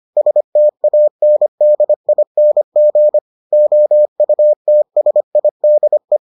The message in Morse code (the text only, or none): STANDING OUTSIDE